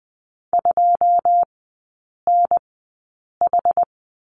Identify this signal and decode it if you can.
Morse code, heard as a keyed tone, about 20 wpm, spaced as for 8 wpm, 690 Hz. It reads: 2NH
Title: 081 Lesson 14e - Number 2 - Three Characters 20wpm